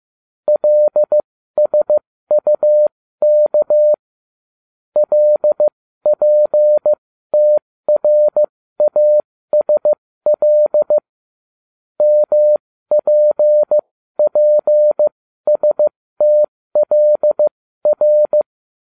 Morse code, heard as a keyed tone, about 15 wpm, spaced as for 13 wpm, 600 Hz. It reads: LSUK LPTRASL MPPSTLR